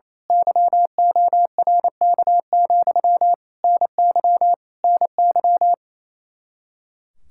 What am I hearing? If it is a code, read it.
Morse code, heard as a keyed tone, about 27 wpm, 700 Hz. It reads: YORK, NY NY
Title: Speed-Racing Common QSO Phrases - 28wpm